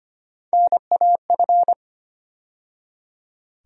Morse code, heard as a keyed tone, about 25 wpm, 700 Hz. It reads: NAF
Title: USA Call Sign Postfixes - 3 Letters 25wpm